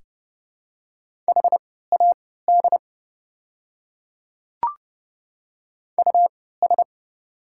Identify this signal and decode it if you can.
Morse code, heard as a keyed tone, about 29 wpm, spaced as for 17 wpm, 700 Hz. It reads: HAD US
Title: 3x Character Spacing - Top 200 Words - 30wpm